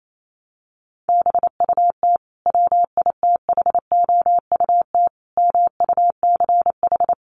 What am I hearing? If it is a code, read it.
Morse code, heard as a keyed tone, about 28 wpm, 700 Hz. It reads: BUT WITHOUT MUCH
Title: Sentences from Top 900 Words 28wpm